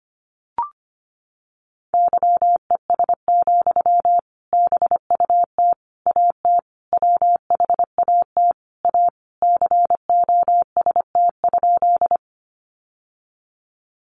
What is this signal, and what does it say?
Morse code, heard as a keyed tone, about 25 wpm, 700 Hz. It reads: YES, BUT AT WHAT A COST?
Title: Sentences from Top 800 Words 25wpm